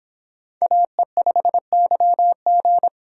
Morse code, heard as a keyed tone, about 26 wpm, 700 Hz. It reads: AE5YG